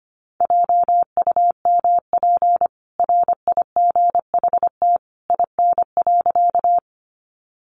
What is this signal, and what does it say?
Morse code, heard as a keyed tone, about 25 wpm, 700 Hz. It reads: JUMP RIGHT IN.
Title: Sentences from Top 1300 Words 25wpm